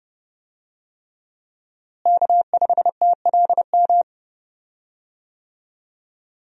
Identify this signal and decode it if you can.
Morse code, heard as a keyed tone, about 30 wpm, 700 Hz. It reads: K5TLM
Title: USA Call Signs - 5 Characters - Repeated 3x - 30wpm